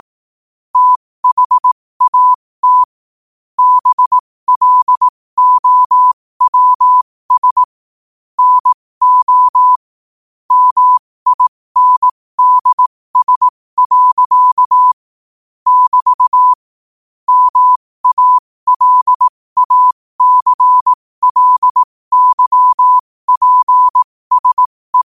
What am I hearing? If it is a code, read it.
Morse code, heard as a keyed tone, about 18 wpm, spaced as for 15 wpm, 1000 Hz. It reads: THAT BLOWS NO MINDS. <BT> MALACLYPSE
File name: qotdfiles / 15wpm